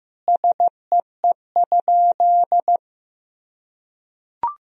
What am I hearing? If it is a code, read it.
Morse code, heard as a keyed tone, about 16 wpm, 700 Hz. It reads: SEE?
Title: Sentences from Top 100 Words and Abbreviations 15wpm